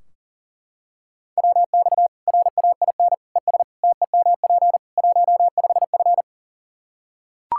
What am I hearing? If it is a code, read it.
Morse code, heard as a keyed tone, about 40 wpm, 700 Hz. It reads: WX RAIN ES TEMP 15F
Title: Practice QSO - Exchange 2 - Info - 40wpm